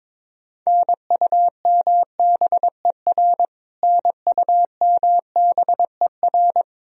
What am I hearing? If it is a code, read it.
Morse code, heard as a keyed tone, about 22 wpm, 700 Hz. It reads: NUMBER NUMBER
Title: Top 100 Words Repeated 22wpm